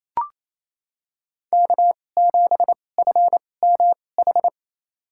Morse code, heard as a keyed tone, about 28 wpm, spaced as for 20 wpm, 700 Hz. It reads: K7FMH
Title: USA Call Signs - 2x Character Spacing - 28wpm